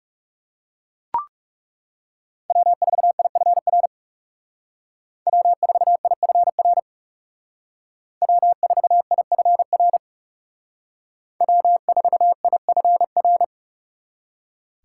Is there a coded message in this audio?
Yes: Morse code from 2.5 to 13.4 s, heard as a keyed tone, about 39 wpm, 700 Hz. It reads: W4IFR W4IFR W4IFR W4IFR